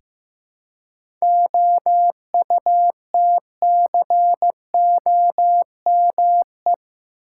Morse code, heard as a keyed tone, about 15 wpm, 700 Hz. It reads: OUTCOME
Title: Seven Letter Words - 15wpm